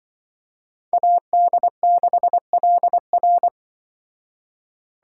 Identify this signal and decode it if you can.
Morse code, heard as a keyed tone, about 24 wpm, 700 Hz. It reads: AD6LR
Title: Speed Racing - USA 2x2 Call Signs 20wpm